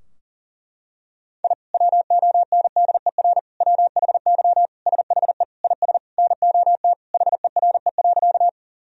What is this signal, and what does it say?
Morse code, heard as a keyed tone, about 40 wpm, 700 Hz. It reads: I WONDER WHY SHE IS NOT HERE.
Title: Sentences from Top 1000 Words - Review Part 3 - 40wpm